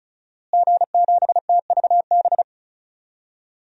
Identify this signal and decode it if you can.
Morse code, heard as a keyed tone, about 34 wpm, 700 Hz. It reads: G7TVB
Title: DXCC-100 Call Signs - 5 Characters - Repeated 3x - 35wpm